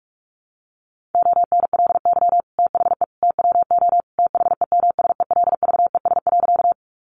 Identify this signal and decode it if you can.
Morse code, heard as a keyed tone, about 44 wpm, 700 Hz. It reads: ONLY THE TWO THEMSELVES.
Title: Sentences from Top 800 Words 45wpm